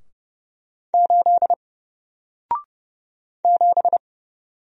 Morse code, heard as a keyed tone, about 30 wpm, 700 Hz. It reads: 8 7
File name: Single-Letter-Number-30wpm